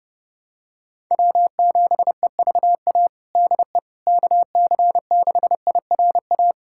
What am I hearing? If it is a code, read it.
Morse code, heard as a keyed tone, about 30 wpm, 700 Hz. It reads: W7EVA DE KC6IRA
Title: USA Callsign DE Callsign 30wpm